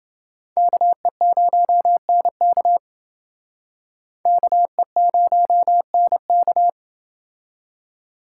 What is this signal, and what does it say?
Morse code, heard as a keyed tone, about 28 wpm, 700 Hz. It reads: KE0NK KE0NK